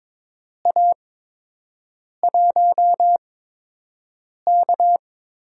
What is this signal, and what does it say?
Morse code, heard as a keyed tone, about 22 wpm, spaced as for 6 wpm, 700 Hz. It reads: A1K
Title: USA Call Signs - 6 Characters - 8x Character Spacing - 22wpm